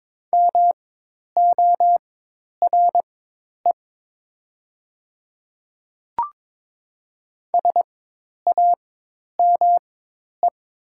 Morse code, heard as a keyed tone, about 22 wpm, spaced as for 10 wpm, 700 Hz. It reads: MORE SAME
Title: 4x Character Spacing - Top 200 Words - 22wpm